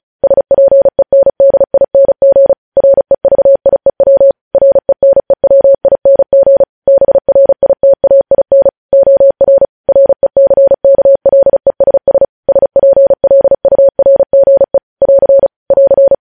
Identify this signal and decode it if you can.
Morse code, heard as a keyed tone, about 36 wpm, 550 Hz. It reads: SPENDING REVIEW RENEWING BRITAIN OR RECKLESS SPLURGE <AR> <AR>
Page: news headlines in morse code at 35 wpm